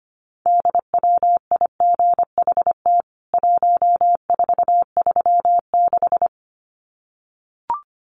Morse code, heard as a keyed tone, about 25 wpm, 700 Hz. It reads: DWIGHT 1436
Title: CWT Exchange 25wpm